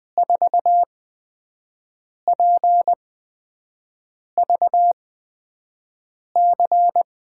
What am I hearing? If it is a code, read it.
Morse code, heard as a keyed tone, about 20 wpm, spaced as for 5 wpm, 700 Hz. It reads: 4PVC
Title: USA Call Signs - 8x Character Spacing - 20wpm